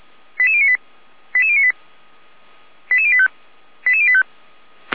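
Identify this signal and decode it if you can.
Five-tone selective call (ZVEI), heard as quick runs of five tones, tones of about 70 ms. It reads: 88098 88098 88085 88085
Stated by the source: ZVEI five-tone selective call received on SDR.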